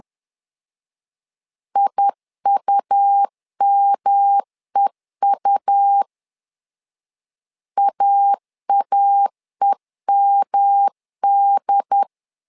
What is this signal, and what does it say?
Morse code, heard as a keyed tone, about 11 wpm, 800 Hz. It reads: IUMEU AAEMD
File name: LEZIONI / 3 MWRO / 3 M 10wpm 1828